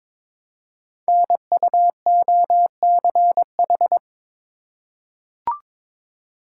Morse code, heard as a keyed tone, about 21 wpm, 700 Hz. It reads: NUOCH
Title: Code Groups - 5 Letters - 22wpm